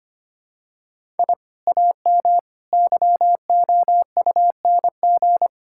Morse code, heard as a keyed tone, about 25 wpm, 700 Hz. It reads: I AM YOUNG
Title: Sentences from Top 1000 Words - Review Part 5 - 25wpm